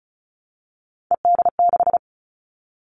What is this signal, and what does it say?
Morse code, heard as a keyed tone, about 34 wpm, 700 Hz. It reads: ED6
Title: Callsign Prefixes for Top 100 DXCC - 35wpm